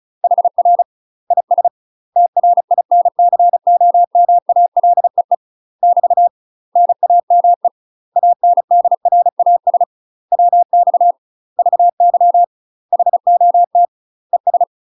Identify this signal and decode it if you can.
Morse code, heard as a keyed tone, about 36 wpm, 700 Hz. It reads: HR IS TRINCOMALEE <BT> NAME ANDRAS WX VY HOT ES